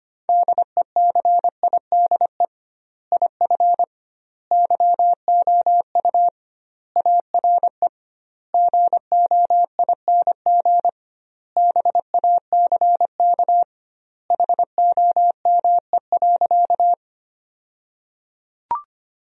Morse code, heard as a keyed tone, about 25 wpm, 700 Hz. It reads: DECIDE IF YOU ARE GOING BACK HOME.
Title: 2x Word-Spacing - Sentences from Top 500 Words - 25wpm